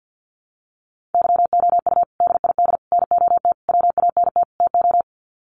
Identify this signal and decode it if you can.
Morse code, heard as a keyed tone, about 50 wpm, 700 Hz. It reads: YOU DID NOT WANT TO
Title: Sets of 5 Words Encore2 50wpm